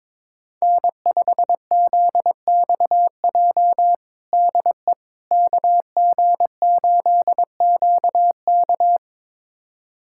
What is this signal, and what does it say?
Morse code, heard as a keyed tone, about 22 wpm, 700 Hz. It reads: N5ZXJ DE KG8QK
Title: USA Callsign DE Callsign - 5 Characters - 22wpm